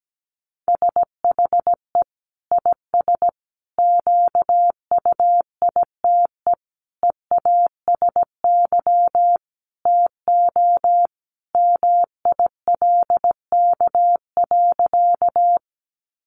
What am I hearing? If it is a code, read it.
Morse code, heard as a keyed tone, about 17 wpm, 700 Hz. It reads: SHE IS QUITE EASY TO MILK.
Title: Sentences from Top 1500 Words 17wpm